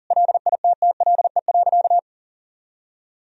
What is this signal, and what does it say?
Morse code, heard as a keyed tone, about 40 wpm, 700 Hz. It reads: LITTLE.
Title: Sentences from Top 1000 Words - Review Part 3 - 40wpm